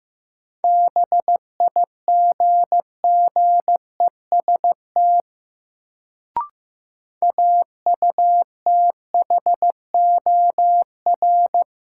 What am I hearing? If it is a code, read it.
Morse code, heard as a keyed tone, about 15 wpm, 700 Hz. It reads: BIGGEST AUTHOR
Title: Sets of 4 Words - New Word Drill - 15wpm